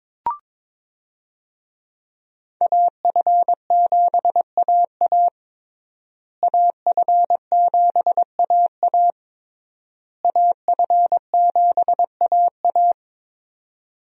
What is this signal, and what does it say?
Morse code, heard as a keyed tone, about 22 wpm, 690 Hz. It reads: AF7AA AF7AA AF7AA
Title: USA Call Signs - 5 Characters - Repeated 3x - 22wpm